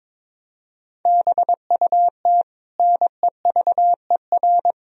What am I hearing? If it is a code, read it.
Morse code, heard as a keyed tone, about 21 wpm, 700 Hz. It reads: BUT NEVER
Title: Sentences from Top 1000 Words - Review Part 1 - 22wpm